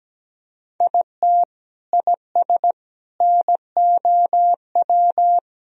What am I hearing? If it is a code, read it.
Morse code, heard as a keyed tone, about 17 wpm, 700 Hz. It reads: IT IS NOW
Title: Sentences from Top 200 Words and Abbreviations 17wpm